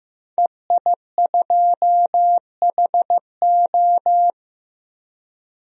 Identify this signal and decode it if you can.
Morse code, heard as a keyed tone, about 16 wpm, 700 Hz. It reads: EI2HO
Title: DXCC-100 Call Signs - 5 Characters - 15wpm